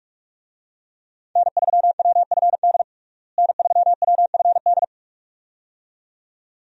Morse code, heard as a keyed tone, about 42 wpm, 700 Hz. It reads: N3WFD N3WFD
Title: USA Call Signs - 5 Characters - Repeated 2x - 45wpm